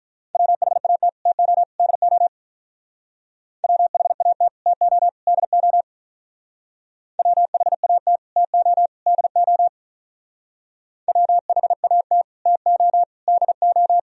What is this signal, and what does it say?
Morse code, heard as a keyed tone, about 45 wpm, 700 Hz. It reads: WHAT TO DO WHAT TO DO WHAT TO DO WHAT TO DO